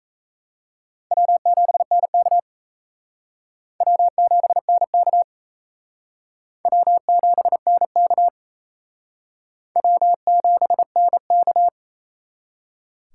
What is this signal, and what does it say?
Morse code, heard as a keyed tone, about 34 wpm, 700 Hz. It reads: W7NK W7NK W7NK W7NK
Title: Speed Racing - USA 1x2 Call Signs 28wpm